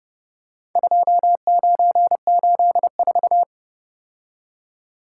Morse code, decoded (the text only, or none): 2984